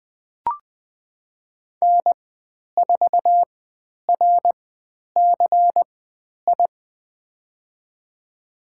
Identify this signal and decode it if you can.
Morse code, heard as a keyed tone, about 20 wpm, spaced as for 10 wpm, 700 Hz. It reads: N4RCI
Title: 095 Lesson 16g - Letter c - Call Signs 20wpm